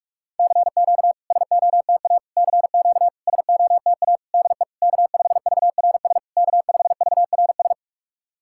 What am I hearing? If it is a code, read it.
Morse code, heard as a keyed tone, about 44 wpm, 690 Hz. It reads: CQ SOTA CQ SOTA DE K5VRS K5VRS
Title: SOTA QSO Practice 45wpm